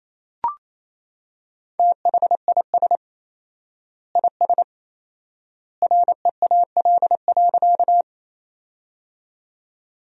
Morse code, heard as a keyed tone, about 27 wpm, 700 Hz. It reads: THIS IS REAL.
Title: 4x Word-Spacing - Sentences from Top 300 Words - 28wpm